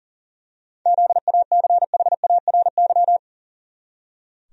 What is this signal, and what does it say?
Morse code, heard as a keyed tone, about 40 wpm, 700 Hz. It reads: ZACHARY
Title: USA Common Names - 40wpm